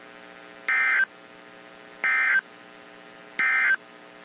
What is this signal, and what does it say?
Emergency Alert System (SAME) tones, heard as harsh buzzing bursts. It is heard from 0.6 to 3.8 s.
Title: WZ2523 - NOAA Weather Radio